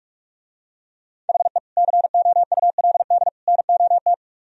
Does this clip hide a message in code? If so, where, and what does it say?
Yes: Morse code from 1.3 to 4.1 s, heard as a keyed tone, about 45 wpm, 700 Hz. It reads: HE COULD NOT